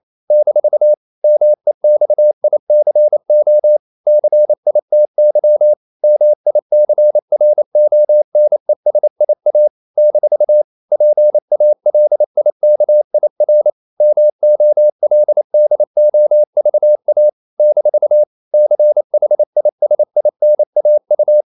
Morse code, heard as a keyed tone, about 28 wpm, 600 Hz. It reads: - MEXICO CITY MICRONESIA - PALIKIR MOLDOVA - CHISINAU
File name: CW / Practice / NationalCapitals28